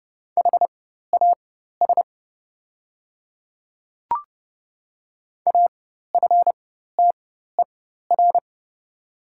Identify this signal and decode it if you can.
Morse code, heard as a keyed tone, about 29 wpm, spaced as for 14 wpm, 700 Hz. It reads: HAS AFTER
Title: 4x Character Spacing - Top 300 Words - 30wpm